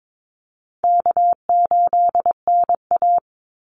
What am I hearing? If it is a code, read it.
Morse code, heard as a keyed tone, about 22 wpm, 700 Hz. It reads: K8NA
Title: Easy Call Signs Repeated 2x - 22wpm